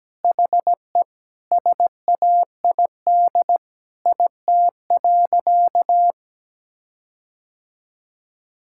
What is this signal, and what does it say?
Morse code, heard as a keyed tone, about 17 wpm, 700 Hz. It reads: HE SAID IT.
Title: Sentences from Top 1600 Words 17wpm